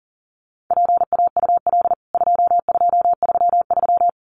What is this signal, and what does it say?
Morse code, heard as a keyed tone, about 40 wpm, 700 Hz. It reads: PAUL 2233
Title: CWT Exchange 40wpm